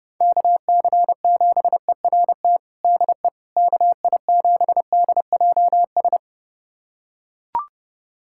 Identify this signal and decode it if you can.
Morse code, heard as a keyed tone, about 30 wpm, 700 Hz. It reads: KC7ERT DE KI7DJS
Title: USA Callsign DE Callsign - 6 Characters - 30wpm